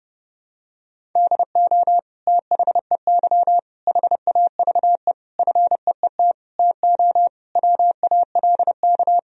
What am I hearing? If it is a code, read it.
Morse code, heard as a keyed tone, about 30 wpm, 700 Hz. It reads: DO THEY HAVE FEET TO WALK
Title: Sentences from Top 1000 Words - Review Part 1 - 30wpm